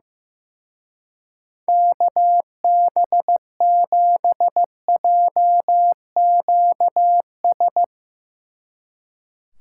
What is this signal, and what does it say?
Morse code, heard as a keyed tone, about 15 wpm, 700 Hz. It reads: KB7JQS
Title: Speed Racing - USA Call Signs 15wpm